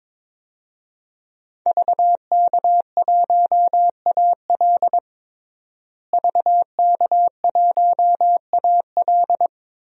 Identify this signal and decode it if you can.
Morse code, heard as a keyed tone, about 22 wpm, 700 Hz. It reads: VK1AL VK1AL
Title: DXCC-100 Call Signs - 5 Characters - Repeated 3x - 22wpm